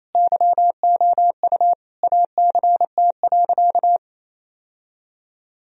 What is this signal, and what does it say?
Morse code, heard as a keyed tone, about 28 wpm, 700 Hz. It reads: YOU ACT.
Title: Sentences from Top 1000 Words - Review Part 1 - 28wpm